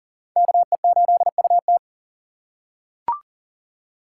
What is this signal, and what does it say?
Morse code, heard as a keyed tone, about 40 wpm, 700 Hz. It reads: KE8UT
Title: USA Call Signs - 5 Characters - 6x Character Spacing - 40wpm